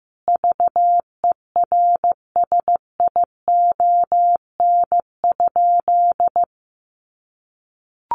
Morse code, heard as a keyed tone, about 15 wpm, 700 Hz. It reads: VERSION?
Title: Sentences from Top 1400 Words 15wpm